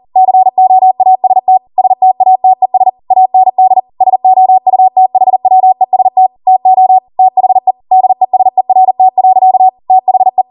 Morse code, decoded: COAST STATES AND SOUTHWEST TO THE DESERT. THE